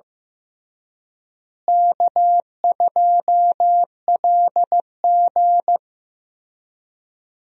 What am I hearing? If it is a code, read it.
Morse code, heard as a keyed tone, about 15 wpm, 700 Hz. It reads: K2LG
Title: USA Call Signs - 4 Characters - Repeated 2x - 15wpm